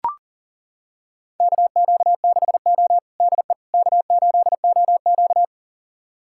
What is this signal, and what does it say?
Morse code, heard as a keyed tone, about 40 wpm, 700 Hz. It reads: KQ6O DE K8OQ
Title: USA Callsign DE Callsign - 4 Characters - 40wpm